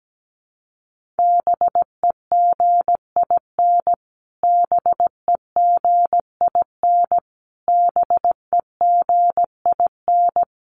Morse code, heard as a keyed tone, about 17 wpm, 700 Hz. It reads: BEGIN BEGIN BEGIN
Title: Top 500 Words Repeated 17wpm